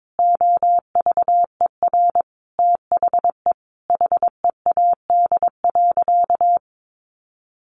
Morse code, heard as a keyed tone, about 22 wpm, 700 Hz. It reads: OVER THE HEAD.
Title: Sentences from Top 1200 Words 22wpm